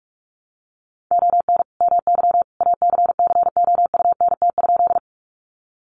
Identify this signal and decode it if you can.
Morse code, heard as a keyed tone, about 44 wpm, 700 Hz. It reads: ON MY ACCOUNT?
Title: Sentences from Top 900 Words 45wpm